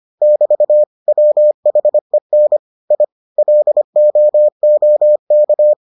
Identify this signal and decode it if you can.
Morse code, heard as a keyed tone, about 25 wpm, 600 Hz. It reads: <BT> WHEN I LOOK